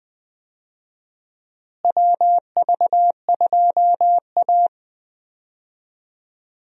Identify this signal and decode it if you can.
Morse code, heard as a keyed tone, about 21 wpm, 700 Hz. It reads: WV2A
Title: USA Call Signs - 4 Characters - Repeated 2x - 20wpm